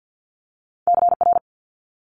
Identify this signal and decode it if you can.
Morse code, heard as a keyed tone, about 50 wpm, 700 Hz. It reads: CR
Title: Multi-letter Phonemes Quick Fire 50wpm